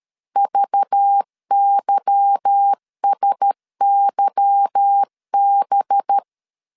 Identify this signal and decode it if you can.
Morse code, heard as a keyed tone, about 13 wpm, 800 Hz. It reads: VYSYB
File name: CW / DET / 1 9 1500C S B 12wpm 3819